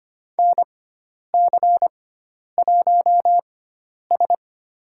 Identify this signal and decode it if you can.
Morse code, heard as a keyed tone, about 25 wpm, spaced as for 10 wpm, 700 Hz. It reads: NC1S
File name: Lesson 16g - Letter c - Call Signs 25wpm